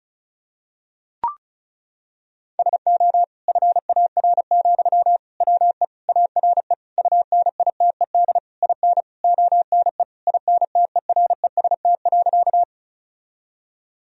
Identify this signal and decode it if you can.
Morse code, heard as a keyed tone, about 36 wpm, 700 Hz. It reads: SO FAR, WE ARE UNITED IN ONE INTEREST.
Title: Sentences from Top 1000 Words - Review Part 4 - 35wpm